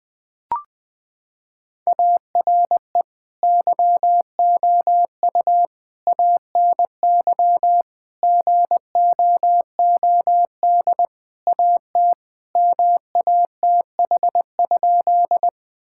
Morse code, heard as a keyed tone, about 19 wpm, 700 Hz. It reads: ARE YOU ANY GOOD AT MATH?
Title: Sentences from Top 1000 Words - Review Part 6 - 20wpm